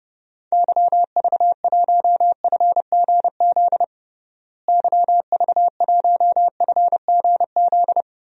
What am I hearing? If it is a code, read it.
Morse code, heard as a keyed tone, about 31 wpm, 700 Hz. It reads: YV1FGZ YV1FGZ
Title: DXCC-100 Call Signs - 6 Characters - Repeated 3x - 30wpm